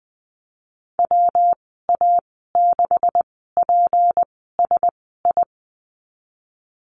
Morse code, heard as a keyed tone, about 20 wpm, spaced as for 15 wpm, 700 Hz. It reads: WA6PSI
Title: 154 Lesson 25g - Letter p - Call Signs 20wpm